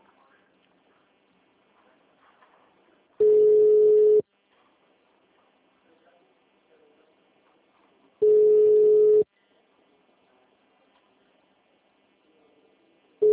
Ringback tone (Europe, 425 Hz).
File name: out- -2001- - - .23575